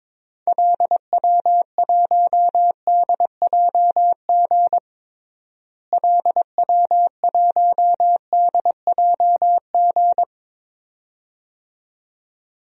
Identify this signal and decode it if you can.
Morse code, heard as a keyed tone, about 22 wpm, 700 Hz. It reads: LW1DJG LW1DJG
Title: DXCC-100 Call Signs - 6 Characters - Repeated 2x - 22wpm